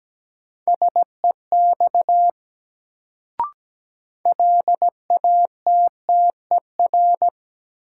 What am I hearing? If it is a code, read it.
Morse code, heard as a keyed tone, about 17 wpm, 700 Hz. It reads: SEX LATTER
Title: Sentences from Top 1600 Words - New Word Drill Alt - 17wpm